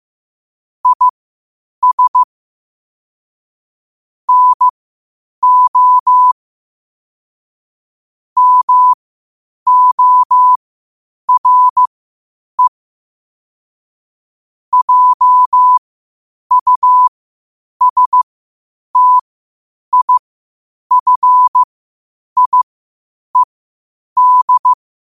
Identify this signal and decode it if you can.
Morse code, heard as a keyed tone, about 15 wpm, spaced as for 8 wpm, 1000 Hz. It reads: IS NO MORE JUSTIFIED
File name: qotdfiles / 8wpm